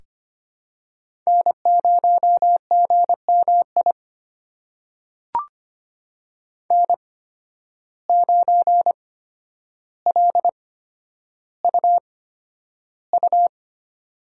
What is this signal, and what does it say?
Morse code, heard as a keyed tone, about 25 wpm, spaced as for 11 wpm, 700 Hz. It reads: N0GMI N9LUU
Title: USA Call Signs - 5 Characters - 8x Character Spacing - 25wpm